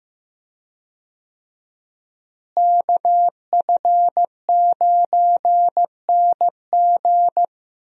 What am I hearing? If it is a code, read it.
Morse code, heard as a keyed tone, about 15 wpm, 700 Hz. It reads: KF9NG
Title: USA Call Signs - 5 Characters - Repeated 3x - 15wpm